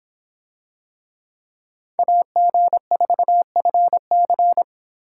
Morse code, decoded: AG4FC

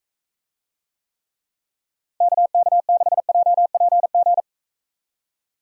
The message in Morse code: KK6JPG

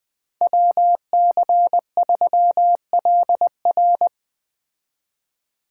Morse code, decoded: WC3LR

